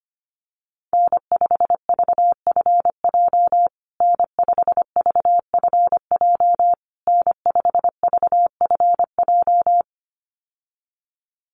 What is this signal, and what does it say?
Morse code, heard as a keyed tone, about 25 wpm, 700 Hz. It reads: N5VFJ N5VFJ N5VFJ
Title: USA Call Signs Repeated 3x 25wpm